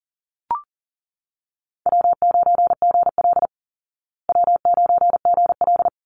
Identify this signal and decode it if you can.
Morse code, heard as a keyed tone, about 40 wpm, 700 Hz. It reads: W9GL W9GL
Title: Easy Call Signs Repeated 2x - 40wpm